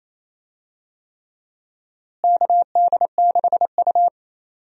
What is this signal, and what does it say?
Morse code, heard as a keyed tone, about 28 wpm, 700 Hz. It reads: KD6U
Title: USA Call Signs - 4 Characters - Repeated 3x - 28wpm